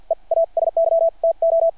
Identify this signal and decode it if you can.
Morse code, heard as a keyed tone, about 50 wpm, spaced as for 40 wpm, 660 Hz. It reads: EASY TO
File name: archived / CRSnet